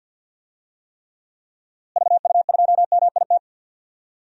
Morse code, heard as a keyed tone, about 47 wpm, 700 Hz. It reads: VU2MIT
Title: DXCC-100 Call Signs - 6 Characters - Repeated 3x - 50wpm